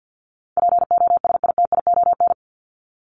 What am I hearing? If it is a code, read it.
Morse code, heard as a keyed tone, about 50 wpm, 700 Hz. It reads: POSITION
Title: Top 751-1000 Words 50wpm